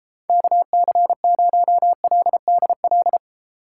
Morse code, heard as a keyed tone, about 33 wpm, 700 Hz. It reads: KC0LDL